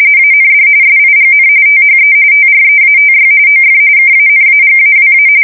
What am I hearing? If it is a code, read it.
RTTY, about 45.45 baud. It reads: AND SIGN THE TREATY OF KANAGAWA
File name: fsk170 / usnavy2